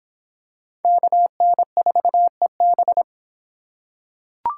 Morse code, heard as a keyed tone, about 23 wpm, 700 Hz. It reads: KN4EB